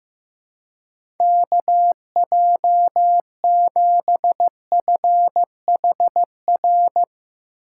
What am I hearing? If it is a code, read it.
Morse code, heard as a keyed tone, about 15 wpm, 700 Hz. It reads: KJ7FHR
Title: USA Call Signs - 6 Characters - 2x Character Spacing - 15wpm